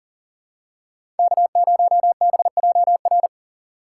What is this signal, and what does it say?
Morse code, heard as a keyed tone, about 40 wpm, 700 Hz. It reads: K0BJR
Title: USA Call Signs - 5 Characters - 3x Character Spacing - 40wpm